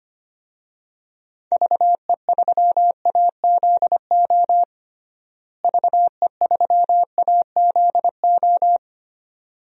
Morse code, heard as a keyed tone, about 25 wpm, 700 Hz. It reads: VE3AZO VE3AZO
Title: DXCC-100 Call Signs - 6 Characters - Repeated 2x - 25wpm